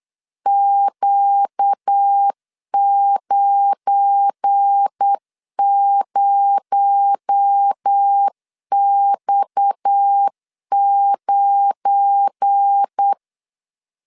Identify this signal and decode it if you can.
Morse code, heard as a keyed tone, about 8 wpm, 800 Hz. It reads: Q90X9
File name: LEZIONI / 7 GF9Z / 7 GF9 8wpm 3545